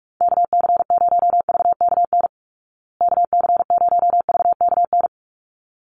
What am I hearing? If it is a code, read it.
Morse code, heard as a keyed tone, about 44 wpm, 700 Hz. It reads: KC0VKN KC0VKN
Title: Call Signs Repeated 3x - 45wpm